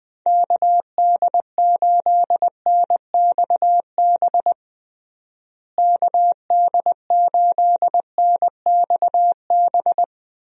Morse code, heard as a keyed tone, about 20 wpm, 700 Hz. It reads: KD8NXB KD8NXB
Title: USA Call Signs - 6 Characters - Repeated 3x - 20wpm